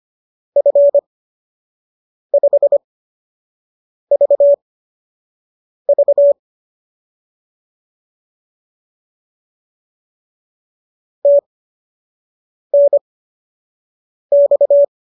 Morse code, heard as a keyed tone, about 25 wpm, spaced as for 6 wpm, 570 Hz. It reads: F5VV TNX